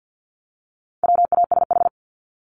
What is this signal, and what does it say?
Morse code, heard as a keyed tone, about 50 wpm, 700 Hz. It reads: WASH
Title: Top 1751-2000 Words 50wpm